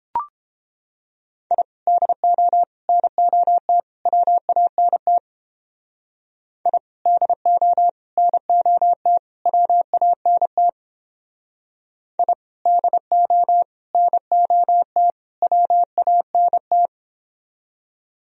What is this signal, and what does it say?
Morse code, heard as a keyed tone, about 29 wpm, 700 Hz. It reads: I DO NOT WANT I DO NOT WANT I DO NOT WANT